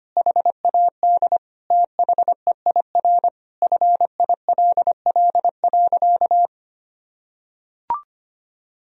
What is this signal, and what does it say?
Morse code, heard as a keyed tone, about 25 wpm, 700 Hz. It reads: HAD THEIR FILL.
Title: Sentences from Top 1000 Words - Review Part 3 - 25wpm